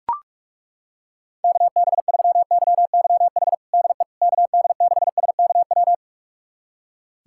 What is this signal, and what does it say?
Morse code, heard as a keyed tone, about 41 wpm, 700 Hz. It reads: KB3YYH DE KD6SKW
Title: USA Callsign DE Callsign - 6 Characters - 45wpm